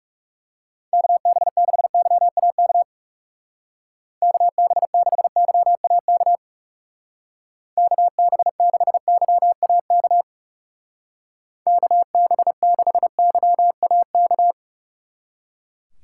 Morse code, heard as a keyed tone, about 36 wpm, 700 Hz. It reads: KB6YAK KB6YAK KB6YAK KB6YAK
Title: Speed Racing - USA Call Signs 30wpm